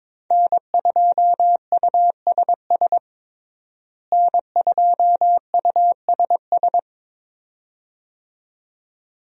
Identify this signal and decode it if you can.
Morse code, heard as a keyed tone, about 22 wpm, 700 Hz. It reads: N2USS N2USS
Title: USA Call Signs - 5 Characters - Repeated 2x - 22wpm